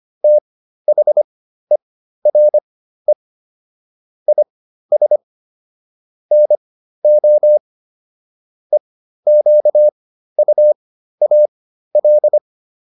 Morse code, heard as a keyed tone, about 25 wpm, spaced as for 13 wpm, 600 Hz. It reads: THERE IS NO EQUAL